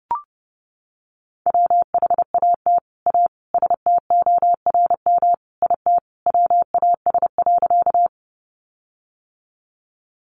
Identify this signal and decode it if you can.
Morse code, heard as a keyed tone, about 30 wpm, 700 Hz. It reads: WHAT A STORM IT WAS.
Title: Sentences from Top 1500 Words 30wpm